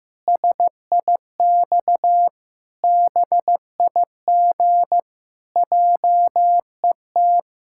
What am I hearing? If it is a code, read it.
Morse code, heard as a keyed tone, about 15 wpm, 700 Hz. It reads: SIX BIG JET
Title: Pangram Challenge v2 - 15wpm